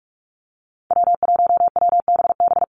Morse code, heard as a keyed tone, about 44 wpm, 700 Hz. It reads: W1WBB
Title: Call Signs Repeated 2x - 45wpm